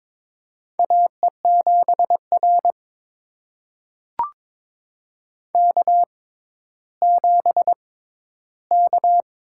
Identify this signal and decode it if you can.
Morse code, heard as a keyed tone, about 22 wpm, spaced as for 12 wpm, 700 Hz. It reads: AE7R K7K
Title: USA Call Signs - 4 Characters - 6x Character Spacing - 22wpm